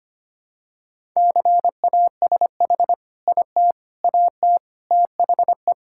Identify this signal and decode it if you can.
Morse code, heard as a keyed tone, about 25 wpm, 700 Hz. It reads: CASH IT AT THE
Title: Sentences from Top 1000 Words - Review Part 5 - 25wpm